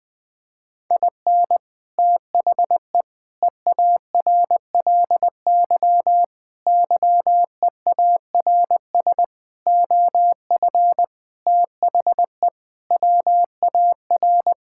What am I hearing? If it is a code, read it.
Morse code, heard as a keyed tone, about 20 wpm, 700 Hz. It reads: IN THE EARLY YEARS OF THE WAR
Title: Sentences with 6-Word Phrases - 20wpm